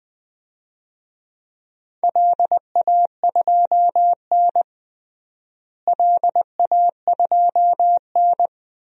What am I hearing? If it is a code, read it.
Morse code, heard as a keyed tone, about 21 wpm, 700 Hz. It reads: LA2N LA2N
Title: DXCC-100 Call Signs - 4 Characters - Repeated 2x - 20wpm